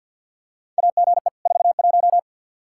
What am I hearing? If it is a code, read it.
Morse code, heard as a keyed tone, about 49 wpm, 700 Hz. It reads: AGE 41
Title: Practice QSO - Exchange 3 - More Info - 50wpm